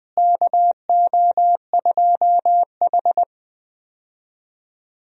Morse code, heard as a keyed tone, about 20 wpm, 700 Hz. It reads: KO2H
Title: USA Call Signs - 4 Characters - Repeated 3x - 20wpm